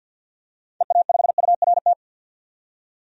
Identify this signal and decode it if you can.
Morse code, heard as a keyed tone, about 52 wpm, 700 Hz. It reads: EA5URT
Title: DXCC-100 Call Signs - 6 Characters - 50wpm